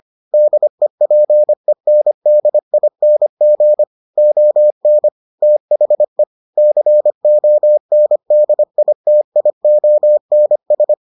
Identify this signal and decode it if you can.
Morse code, heard as a keyed tone, about 25 wpm, 600 Hz. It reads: DEPENDING ON THE CONDITIONS